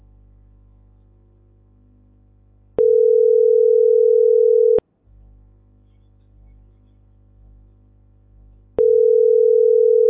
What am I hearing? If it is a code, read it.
Ringback tone (North America), 440 + 480 Hz.